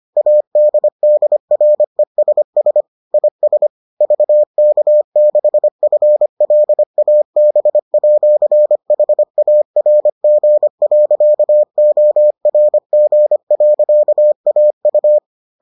Morse code, heard as a keyed tone, about 25 wpm, 600 Hz. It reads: ADDRESS IS VK6FLAB@HARG.ORG.AU